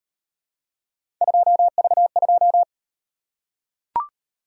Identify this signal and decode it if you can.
Morse code, heard as a keyed tone, about 37 wpm, 700 Hz. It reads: 2V2